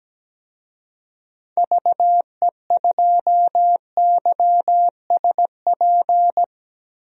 Morse code, heard as a keyed tone, about 17 wpm, 700 Hz. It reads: VE2YSP
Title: DXCC-100 Call Signs - 6 Characters - Repeated 2x - 17wpm